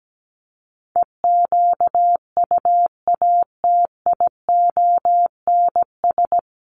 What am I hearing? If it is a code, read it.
Morse code, heard as a keyed tone, about 17 wpm, 700 Hz. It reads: EQUATIONS
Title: Sets of 5 Words 17wpm